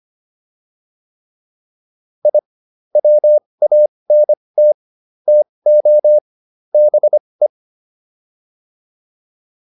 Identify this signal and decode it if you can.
Morse code, heard as a keyed tone, about 25 wpm, spaced as for 20 wpm, 600 Hz. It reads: I WANT TO BE